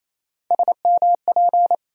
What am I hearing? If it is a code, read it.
Morse code, heard as a keyed tone, about 28 wpm, 700 Hz. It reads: SMP
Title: USA Call Sign Postfixes - 3 Letters 28wpm